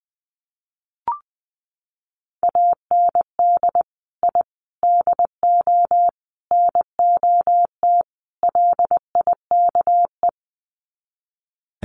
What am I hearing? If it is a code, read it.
Morse code, heard as a keyed tone, about 20 wpm, 700 Hz. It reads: AND I DO NOT LIKE
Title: Sets of 5 Words Encore2 20wpm